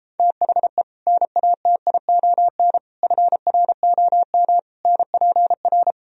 Morse code, heard as a keyed tone, about 31 wpm, 700 Hz. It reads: THE NATION FROM NPR